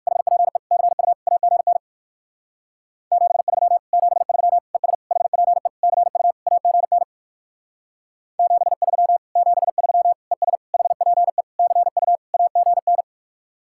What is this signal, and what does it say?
Morse code, heard as a keyed tone, about 65 wpm, 700 Hz. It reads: HPE CU AGN 73 73 ES HPE CU AGN 73 73 ES HPE CU AGN